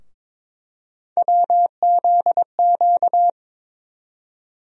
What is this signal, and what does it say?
Morse code, heard as a keyed tone, about 22 wpm, 700 Hz. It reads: WZQ
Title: USA Call Sign Postfixes - 3 Letters 22wpm